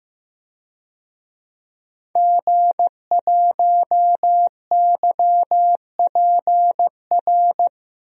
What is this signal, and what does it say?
Morse code, heard as a keyed tone, about 15 wpm, 700 Hz. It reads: G1YPR
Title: DXCC-100 Call Signs - 5 Characters - 15wpm